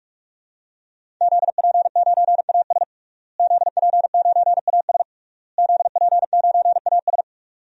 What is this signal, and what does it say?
Morse code, heard as a keyed tone, about 46 wpm, 700 Hz. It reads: ZP9AS ZP9AS ZP9AS
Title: DXCC-100 Call Signs - 5 Characters - Repeated 3x - 45wpm